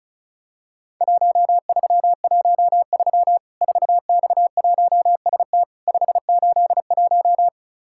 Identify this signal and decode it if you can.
Morse code, heard as a keyed tone, about 36 wpm, 700 Hz. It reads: 1313 4X1ST 581
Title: Worked All Europe DX Contest - QTC Exchange - 35wpm